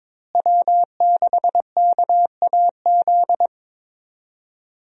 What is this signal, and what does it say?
Morse code, heard as a keyed tone, about 22 wpm, 700 Hz. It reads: W6KAZ
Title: USA Call Signs - 5 Characters - Repeated 3x - 22wpm